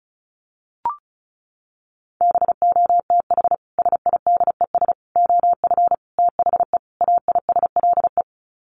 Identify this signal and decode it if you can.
Morse code, heard as a keyed tone, about 34 wpm, 700 Hz. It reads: BOTH SIDES OF THE AISLE
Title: Sets of 5 Words Encore2 35wpm